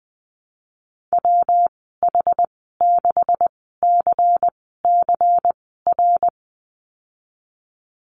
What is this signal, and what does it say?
Morse code, heard as a keyed tone, about 20 wpm, spaced as for 15 wpm, 700 Hz. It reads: WH6CCR
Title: 173 Lesson 28g - Number 9 - Call Signs 20wpm